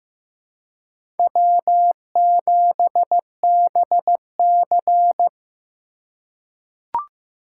Morse code, heard as a keyed tone, about 15 wpm, 700 Hz. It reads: W7BC
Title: USA Call Signs - 4 Characters - 3x Character Spacing - 15wpm